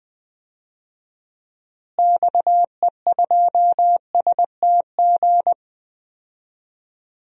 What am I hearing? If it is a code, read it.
Morse code, heard as a keyed tone, about 20 wpm, 700 Hz. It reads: XE2STG
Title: DXCC-100 Call Signs - 6 Characters - Repeated 2x - 20wpm